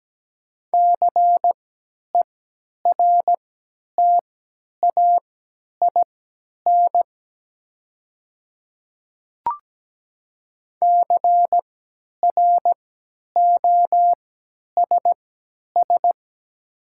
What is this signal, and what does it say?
Morse code, heard as a keyed tone, about 17 wpm, spaced as for 10 wpm, 700 Hz. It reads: CERTAIN CROSS
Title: 3x Character Spacing - Top 500 Words - 17wpm